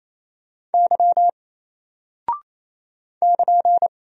Morse code, heard as a keyed tone, about 27 wpm, 700 Hz. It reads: Y <KN>
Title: Single Character (Letter:Number:Punctuation:Prosign) 28wpm